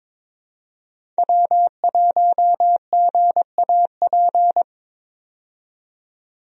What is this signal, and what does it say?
Morse code, heard as a keyed tone, about 22 wpm, 700 Hz. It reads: W1GAP
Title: USA Call Signs - 5 Characters - 22wpm